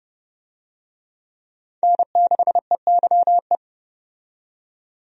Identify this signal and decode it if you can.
Morse code, heard as a keyed tone, about 30 wpm, 700 Hz. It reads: N6EYE
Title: USA Call Signs - 5 Characters - 30wpm